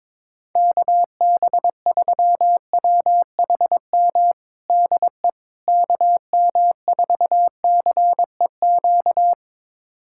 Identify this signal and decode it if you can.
Morse code, heard as a keyed tone, about 22 wpm, 700 Hz. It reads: KB3WHM DE KM4CEQ
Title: USA Callsign DE Callsign - 6 Characters - 22wpm